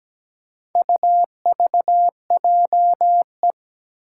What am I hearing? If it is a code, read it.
Morse code, heard as a keyed tone, about 17 wpm, 700 Hz. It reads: UVJE
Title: Four Challenging Letter-Number Combinations 17wpm